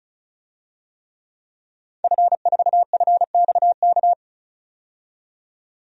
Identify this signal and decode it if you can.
Morse code, heard as a keyed tone, about 36 wpm, 700 Hz. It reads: F4FXK
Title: DXCC-100 Call Signs - 5 Characters - 35wpm